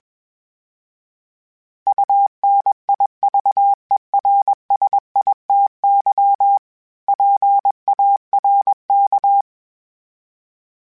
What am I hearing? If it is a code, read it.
Morse code, heard as a keyed tone, about 21 wpm, 800 Hz. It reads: UNIVERSITY PARK
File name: fromSMs / FromSMs mp3s / Cities@20